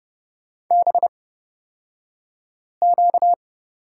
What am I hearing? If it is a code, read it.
Morse code, heard as a keyed tone, about 30 wpm, spaced as for 5 wpm, 700 Hz. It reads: BQ
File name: Lesson 38d - Letter q - Two Characters 30wpm